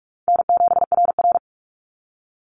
Morse code, heard as a keyed tone, about 44 wpm, 700 Hz. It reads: N7RR
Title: NA SOTA Activator Callsigns 45wpm